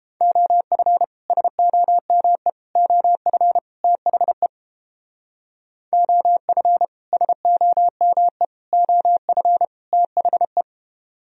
Morse code, heard as a keyed tone, about 31 wpm, 700 Hz. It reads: OF SOME OF THE OF SOME OF THE